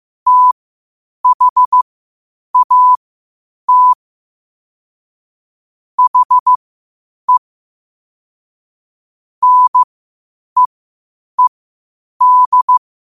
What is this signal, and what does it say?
Morse code, heard as a keyed tone, about 15 wpm, spaced as for 8 wpm, 1000 Hz. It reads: THAT HE NEED
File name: qotdfiles / 8wpm